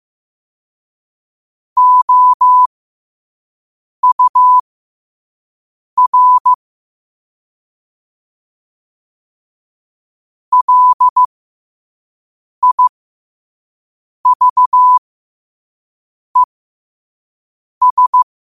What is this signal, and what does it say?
Morse code, heard as a keyed tone, about 15 wpm, spaced as for 5 wpm, 1000 Hz. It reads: OUR LIVES
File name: qotdfiles / 5wpm